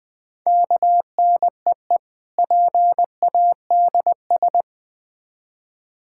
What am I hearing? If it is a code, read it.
Morse code, heard as a keyed tone, about 23 wpm, 700 Hz. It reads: KNEE PADS
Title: Difficult Words - Speed-Racing with Hints - 20wpm